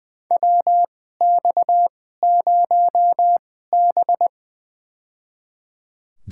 Morse code, heard as a keyed tone, about 20 wpm, spaced as for 14 wpm, 700 Hz. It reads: WX0B
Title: USA Call Signs - 4 Characters - 2x Character Spacing - 20wpm